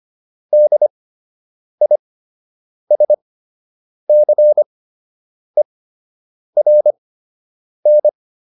Morse code, heard as a keyed tone, about 25 wpm, spaced as for 8 wpm, 600 Hz. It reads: DISCERN